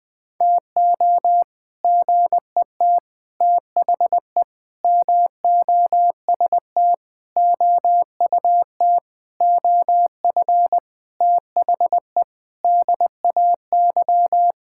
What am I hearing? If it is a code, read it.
Morse code, heard as a keyed tone, about 21 wpm, 700 Hz. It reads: TO GET THE MOST OUT OF THE DAY
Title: Sentences with 7-Word Phrases - 20wpm